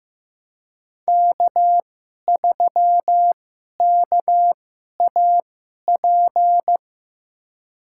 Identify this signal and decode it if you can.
Morse code, heard as a keyed tone, about 16 wpm, spaced as for 11 wpm, 700 Hz. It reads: K3KAP
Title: USA Call Signs - 2x Character Spacing - 15wpm